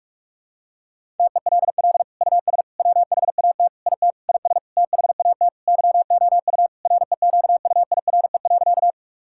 Morse code, heard as a keyed tone, about 46 wpm, 700 Hz. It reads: TELL US WHAT IT IS THAT YOU REQUIRE.
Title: Sentences from Top 1000 Words - Review Part 6 - 45wpm